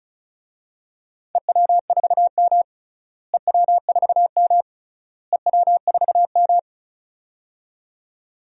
Morse code, heard as a keyed tone, about 36 wpm, 700 Hz. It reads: EW4M EW4M EW4M
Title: DXCC-100 Call Signs - 4 Characters - Repeated 3x - 35wpm